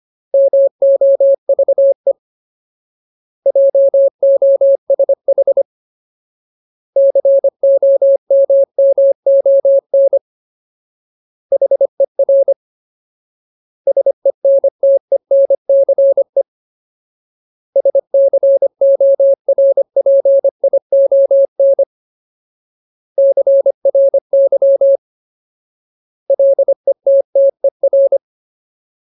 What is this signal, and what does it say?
Morse code, heard as a keyed tone, about 25 wpm, 550 Hz. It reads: MOVE JOSH COMMON HER SENTENCE SCORPION CRY LETTER
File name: Ultimate Word List / 25WPM Ultimate